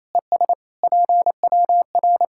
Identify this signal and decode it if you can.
Morse code, heard as a keyed tone, about 28 wpm, 700 Hz. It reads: ES PWR
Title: Practice QSO - Exchange 2 - Info - 28wpm